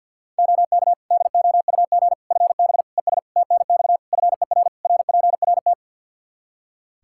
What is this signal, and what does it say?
Morse code, heard as a keyed tone, about 52 wpm, 700 Hz. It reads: OK DOUG FB ES TNX FER RPRT
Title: Practice QSO - Exchange 2 - Info - 50wpm